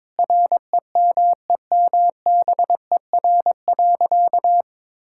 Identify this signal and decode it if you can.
Morse code, heard as a keyed tone, about 22 wpm, 700 Hz. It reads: REMEMBER.
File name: practice / 22wpm / Sentences with 8-Word Phrases - 22wpm